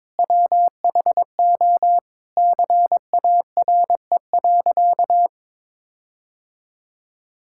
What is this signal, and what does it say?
Morse code, heard as a keyed tone, about 22 wpm, 700 Hz. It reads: WHO CARE.
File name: Sentences with 9-Word Phrases - 22wpm